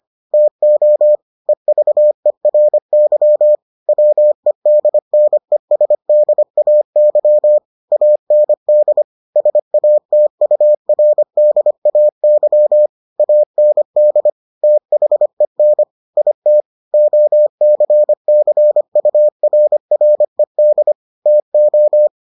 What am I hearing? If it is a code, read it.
Morse code, heard as a keyed tone, about 25 wpm, 600 Hz. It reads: TO EVERY WEDNESDAY AND SATURDAY AND THEN IT OCCURRED TO